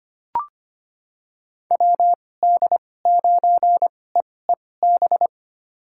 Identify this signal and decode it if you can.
Morse code, heard as a keyed tone, about 25 wpm, spaced as for 18 wpm, 700 Hz. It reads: WD9EEB
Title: USA Call Signs - 6 Characters - 2x Character Spacing - 25wpm